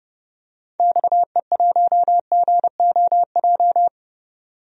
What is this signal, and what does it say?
Morse code, heard as a keyed tone, about 31 wpm, 700 Hz. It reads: XE1GOJ
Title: DXCC-100 Call Signs - 6 Characters - 30wpm